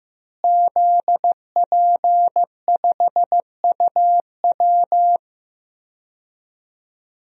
Morse code, heard as a keyed tone, about 15 wpm, 700 Hz. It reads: ZP5UW
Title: DXCC-100 Call Signs - 5 Characters - Repeated 2x - 15wpm